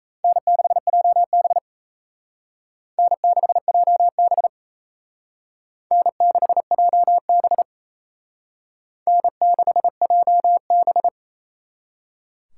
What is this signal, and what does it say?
Morse code, heard as a keyed tone, about 35 wpm, 700 Hz. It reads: N6JB N6JB N6JB N6JB
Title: Speed Racing - USA 1x2 Call Signs 28wpm